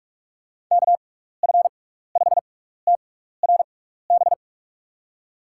Morse code, heard as a keyed tone, about 46 wpm, spaced as for 16 wpm, 700 Hz. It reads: KF5TRB
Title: USA Call Signs - 6 Characters - 6x Character Spacing - 45wpm